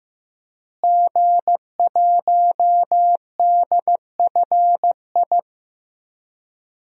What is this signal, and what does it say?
Morse code, heard as a keyed tone, about 15 wpm, 700 Hz. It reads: G1DFI
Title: DXCC-100 Call Signs - 5 Characters - Repeated 3x - 15wpm